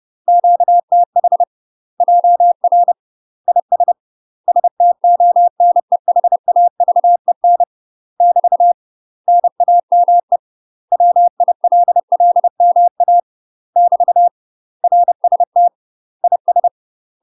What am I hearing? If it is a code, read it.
Morse code, heard as a keyed tone, about 30 wpm, 700 Hz. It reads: QTH JR IS STONEHAVEN <BT> NAME WILLMA <BT> RST IS
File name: UK MASTER QSOs TRAINING / 30WPM / UK MASTER QSOs 0015